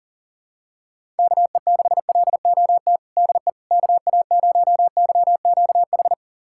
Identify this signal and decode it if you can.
Morse code, heard as a keyed tone, about 40 wpm, 700 Hz. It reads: KE6LOT DE KA0YQH
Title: USA Callsign DE Callsign 40wpm